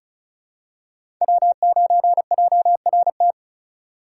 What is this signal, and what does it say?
Morse code, heard as a keyed tone, about 36 wpm, 690 Hz. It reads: W9JRT
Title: USA Call Signs - 5 Characters - 2x Character Spacing - 35wpm